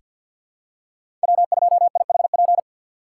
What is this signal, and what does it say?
Morse code, heard as a keyed tone, about 50 wpm, 700 Hz. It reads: W2IHP
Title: USA Call Signs - 5 Characters - 6x Character Spacing - 50wpm